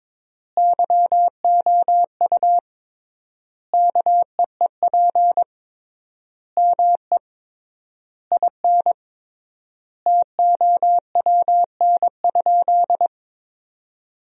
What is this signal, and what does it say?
Morse code, heard as a keyed tone, about 22 wpm, 700 Hz. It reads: YOU KEEP ME IN TOWN?
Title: 3x Word-Spacing - Sentences from Top 500 Words - 22wpm